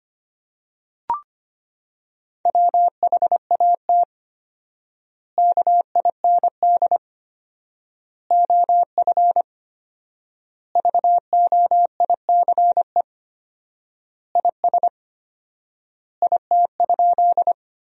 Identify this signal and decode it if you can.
Morse code, heard as a keyed tone, about 25 wpm, 700 Hz. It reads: WHAT KIND OF VOICE IS IT?
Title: 4x Word-Spacing - Sentences from Top 500 Words - 25wpm